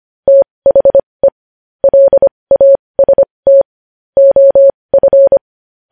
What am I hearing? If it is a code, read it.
Morse code, heard as a keyed tone, about 25 wpm, spaced as for 20 wpm, 550 Hz. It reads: THE LAST OF
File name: podcast / News headlines-1x-25x20